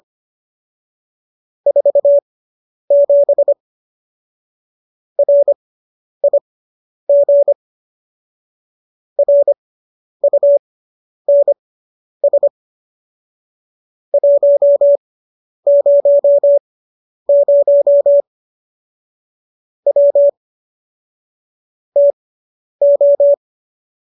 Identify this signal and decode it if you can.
Morse code, heard as a keyed tone, about 25 wpm, spaced as for 10 wpm, 570 Hz. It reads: 47 RIG RUNS 100 W TO